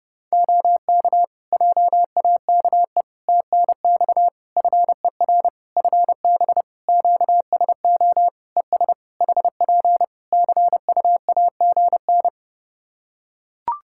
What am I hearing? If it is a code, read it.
Morse code, heard as a keyed tone, about 30 wpm, 700 Hz. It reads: OK JAKE TNX FER FB QSO ES HP CUAGN
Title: Practice QSO - Exchange 4 - Ending - 30wpm